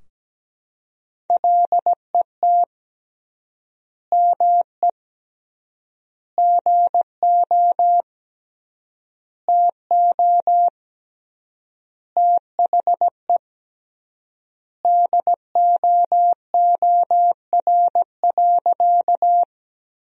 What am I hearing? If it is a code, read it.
Morse code, heard as a keyed tone, about 17 wpm, 700 Hz. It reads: LET ME GO TO THE DOOR.
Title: 3x Word-Spacing - Sentences from Top 300 Words - 17wpm